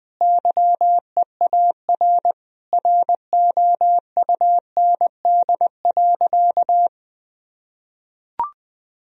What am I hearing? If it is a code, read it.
Morse code, heard as a keyed tone, about 20 wpm, 700 Hz. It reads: YEAR ROUND.
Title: Sentences from Top 1000 Words - Review Part 1 - 20wpm